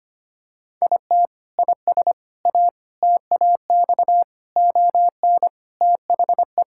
Morse code, heard as a keyed tone, about 25 wpm, 700 Hz. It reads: IT IS A TAX ON THE
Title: Sentences from Top 1000 Words - Review Part 5 - 25wpm